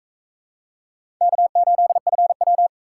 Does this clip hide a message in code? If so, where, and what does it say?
Yes: Morse code from 1.2 to 2.7 s, heard as a keyed tone, about 42 wpm, 700 Hz. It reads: K8FW